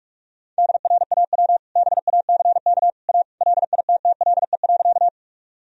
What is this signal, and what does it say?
Morse code, heard as a keyed tone, about 46 wpm, 700 Hz. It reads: DRAW BACK A LITTLE.
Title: Sentences from Top 1000 Words - Review Part 1 - 45wpm